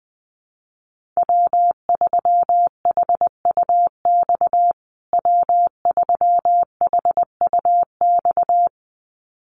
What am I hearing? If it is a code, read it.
Morse code, heard as a keyed tone, about 20 wpm, 700 Hz. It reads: W3HUX W3HUX
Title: USA Call Signs Repeated 2x 20wpm